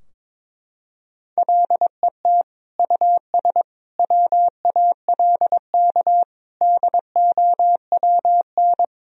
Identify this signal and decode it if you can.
Morse code, heard as a keyed tone, about 22 wpm, 700 Hz. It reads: LET US WALK DOWN
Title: Sentences from Top 1000 Words - Review Part 3 - 22wpm